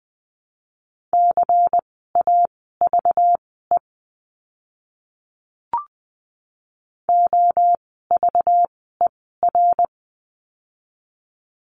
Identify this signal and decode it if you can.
Morse code, heard as a keyed tone, about 20 wpm, spaced as for 15 wpm, 700 Hz. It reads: CAVE OVER
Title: 192 Lesson 31f - Letter v - Words 20wpm